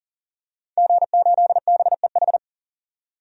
Morse code, heard as a keyed tone, about 40 wpm, 700 Hz. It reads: G8BEH
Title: DXCC-100 Call Signs - 5 Characters - 40wpm